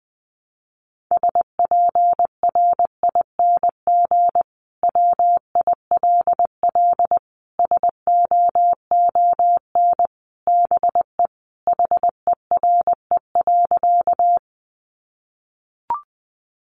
Morse code, heard as a keyed tone, about 20 wpm, 700 Hz. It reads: SPRING WILL SOON BE HERE.
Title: Sentences from Top 1200 Words 20wpm